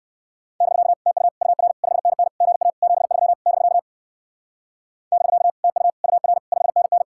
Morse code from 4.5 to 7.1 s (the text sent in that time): <BK> TU UR 5NN